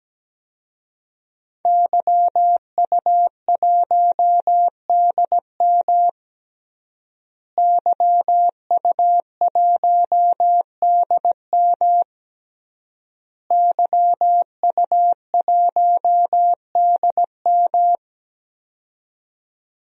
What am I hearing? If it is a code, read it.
Morse code, heard as a keyed tone, about 17 wpm, 700 Hz. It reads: YU1DM YU1DM YU1DM
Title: DXCC-100 Call Signs - 5 Characters - Repeated 3x - 17wpm